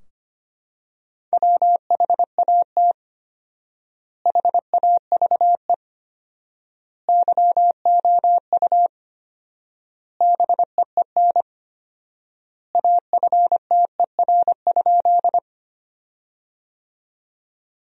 Morse code, heard as a keyed tone, about 25 wpm, 700 Hz. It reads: WHAT HAVE YOU BEEN AFTER?
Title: 4x Word-Spacing - Sentences from Top 200 Words - 25wpm